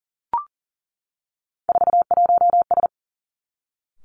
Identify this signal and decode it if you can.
Morse code, heard as a keyed tone, about 40 wpm, 690 Hz. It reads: 41S
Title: Three Challenging Letter-Number Combinations 40wpm